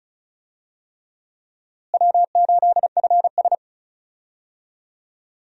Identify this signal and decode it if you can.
Morse code, heard as a keyed tone, about 36 wpm, 700 Hz. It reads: W8FS
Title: USA Call Signs - 4 Characters - Repeated 3x - 35wpm